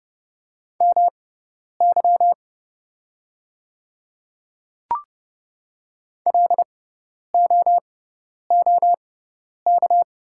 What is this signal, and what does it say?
Morse code, heard as a keyed tone, about 30 wpm, spaced as for 10 wpm, 700 Hz. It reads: MY LOOK
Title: 6x Character Spacing - Top 100 Words - 30wpm